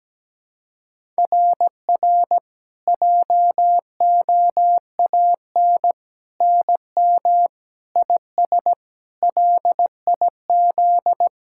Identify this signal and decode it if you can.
Morse code, heard as a keyed tone, about 17 wpm, 700 Hz. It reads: RR JOAN NM IS LIZ
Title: SOTA QSO Practice 17wpm